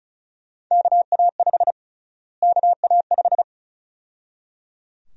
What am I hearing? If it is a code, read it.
Morse code, heard as a keyed tone, about 34 wpm, 700 Hz. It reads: KA5 KA5
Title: USA Call Sign Prefixes - Repeated 2x - 35wpm